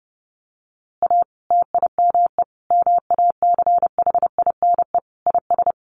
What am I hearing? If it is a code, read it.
Morse code, heard as a keyed tone, about 28 wpm, 700 Hz. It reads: A TIME MACHINE IS
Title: A Sound Of Thunder 30wpm